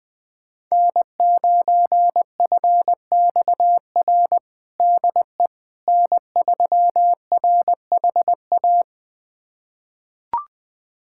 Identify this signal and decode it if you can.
Morse code, heard as a keyed tone, about 21 wpm, 700 Hz. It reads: N9FXR DE N3RHA
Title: USA Callsign DE Callsign - 5 Characters - 20wpm